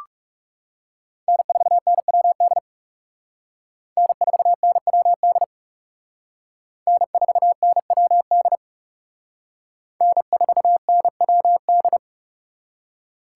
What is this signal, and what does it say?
Morse code, heard as a keyed tone, about 37 wpm, 700 Hz. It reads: N4NWD N4NWD N4NWD N4NWD
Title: Speed Racing - USA Call Signs 30wpm